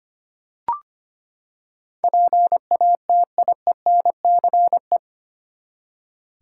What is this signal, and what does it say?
Morse code, heard as a keyed tone, about 25 wpm, 700 Hz. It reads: PATIENCE
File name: Eight Letter Words - 25wpm